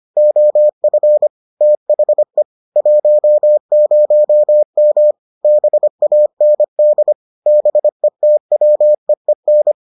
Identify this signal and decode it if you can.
Morse code, heard as a keyed tone, about 25 wpm, 600 Hz. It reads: OF THE 10M BAND BETWEEN